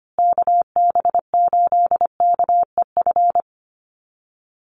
Morse code, heard as a keyed tone, about 25 wpm, 700 Hz. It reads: KB8KEF
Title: USA Call Signs Repeated 3x 25wpm